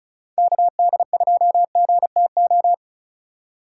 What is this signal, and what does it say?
Morse code, heard as a keyed tone, about 42 wpm, 700 Hz. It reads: KD2GTO
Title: Speed Racing - USA 2x3 Call Signs 35wpm